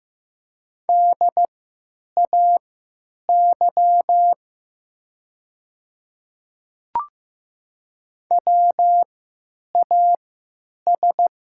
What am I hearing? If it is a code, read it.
Morse code, heard as a keyed tone, about 15 wpm, spaced as for 9 wpm, 700 Hz. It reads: DAY WAS
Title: 3x Character Spacing - Top 100 Words - 15wpm